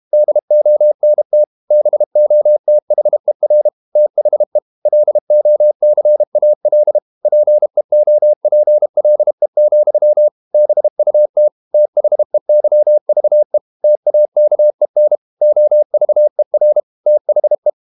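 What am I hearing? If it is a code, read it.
Morse code, heard as a keyed tone, about 32 wpm, 600 Hz. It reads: DONT BOTHER THE LOCAL PEOPLE, BUT THEYVE TAKEN OVER THE
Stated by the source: Morse code practice at 32 wpm from CW med Gnister, reading Ken Follett's novel Never in English.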